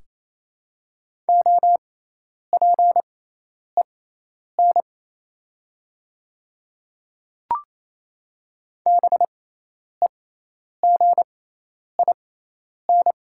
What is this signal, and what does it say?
Morse code, heard as a keyed tone, about 27 wpm, spaced as for 10 wpm, 700 Hz. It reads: OPEN BEGIN
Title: 6x Character Spacing - Top 300 Words - 28wpm